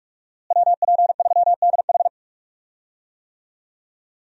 Morse code, heard as a keyed tone, about 46 wpm, 700 Hz. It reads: WP3DH
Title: USA Call Signs - 5 Characters - Repeated 2x - 45wpm